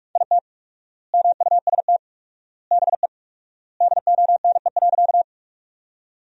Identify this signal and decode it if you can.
Morse code, heard as a keyed tone, about 42 wpm, 700 Hz. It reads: IT MUST BE DONE.
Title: 4x Word-Spacing - Sentences from Top 500 Words - 45wpm